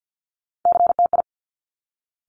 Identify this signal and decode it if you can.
Morse code, heard as a keyed tone, about 50 wpm, 700 Hz. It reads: CTI
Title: Top Three Letter Combinations in Words 50wpm